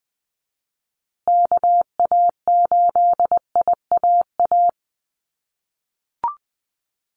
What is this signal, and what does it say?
Morse code, heard as a keyed tone, about 20 wpm, 700 Hz. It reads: KA8IAA
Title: USA Call Signs Repeated 3x 20wpm